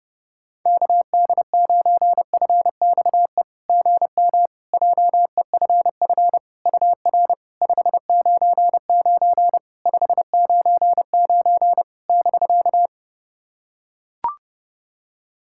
Morse code, heard as a keyed tone, about 30 wpm, 690 Hz. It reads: KD9FXE GM JEFF UR 599 599 <BK>
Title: SOTA QSO Practice 30wpm